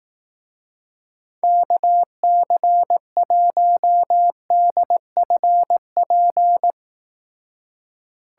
Morse code, heard as a keyed tone, about 18 wpm, 700 Hz. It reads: KC1DFP
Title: Speed Racing - USA 2x3 Call Signs 15wpm